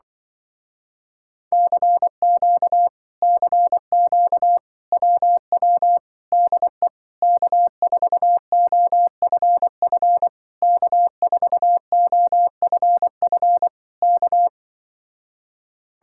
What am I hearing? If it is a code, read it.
Morse code, heard as a keyed tone, about 24 wpm, 700 Hz. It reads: CQ CQ WW DE K4OFF K4OFF K
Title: Speed-Racing Common QSO Phrases Encore - 20wpm